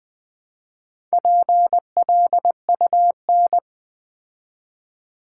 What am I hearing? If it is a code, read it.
Morse code, heard as a keyed tone, about 20 wpm, 700 Hz. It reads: PLUN
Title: Build A Word - 2 Letters At A Time - 6-Letter Words - 20wpm